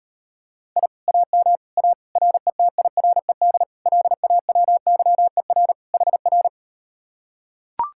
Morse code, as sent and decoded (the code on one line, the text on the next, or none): .. / .- -- / .- / .-. - .. .-. -.. / .-.. .- .-- -.-- . .-. / .... .-.
I AM A RETIRED LAWYER HR